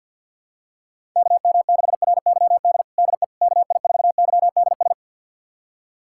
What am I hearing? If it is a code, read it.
Morse code, heard as a keyed tone, about 52 wpm, 700 Hz. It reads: KM6RYD DE KI4YDS
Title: USA Callsign DE Callsign - 6 Characters - 50wpm